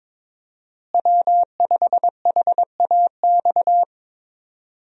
Morse code, heard as a keyed tone, about 22 wpm, 700 Hz. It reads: W5HAX
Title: USA Call Signs - 5 Characters - Repeated 3x - 22wpm